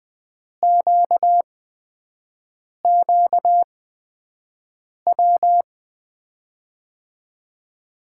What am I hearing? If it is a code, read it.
Morse code, heard as a keyed tone, about 21 wpm, spaced as for 6 wpm, 700 Hz. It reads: QQW
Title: USA Call Signs - 5 Characters - 8x Character Spacing - 20wpm